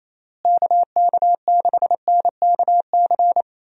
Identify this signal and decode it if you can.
Morse code, heard as a keyed tone, about 28 wpm, 700 Hz. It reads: KK6NKC
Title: USA Call Signs - 6 Characters - Repeated 3x - 28wpm